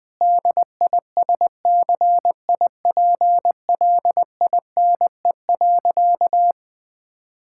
Morse code, heard as a keyed tone, about 20 wpm, 700 Hz. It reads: DISCIPLINE.
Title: Sentences from Top 1000 Words - Review Part 6 - 20wpm